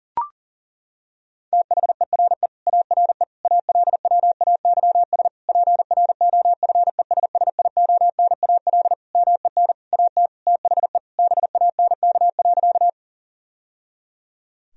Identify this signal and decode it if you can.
Morse code, heard as a keyed tone, about 37 wpm, 700 Hz. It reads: THERE ARE ALWAYS PROFESSIONAL MEN AT THE BANK.
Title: Sentences from Top 1000 Words - Review Part 4 - 40wpm